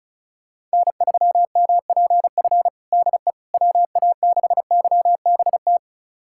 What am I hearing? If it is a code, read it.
Morse code, heard as a keyed tone, about 34 wpm, 700 Hz. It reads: N3MPF DE WA6YBT
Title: USA Callsign DE Callsign 35wpm